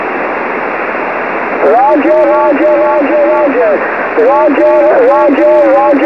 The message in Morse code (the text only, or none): none